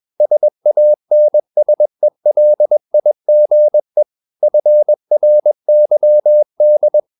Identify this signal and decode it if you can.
Morse code, heard as a keyed tone, about 21 wpm, 600 Hz. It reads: SANSELIGE FRYD
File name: Ferie QRS 21wpm